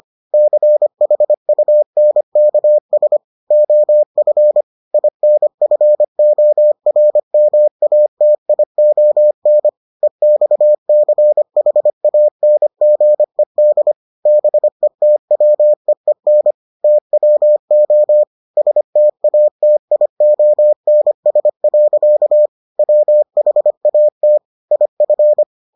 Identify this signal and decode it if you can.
Morse code, heard as a keyed tone, about 25 wpm, 600 Hz. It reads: CHUNKS OF INFORMATION EXCHANGED BETWEEN TWO STATIONS. WHAT IF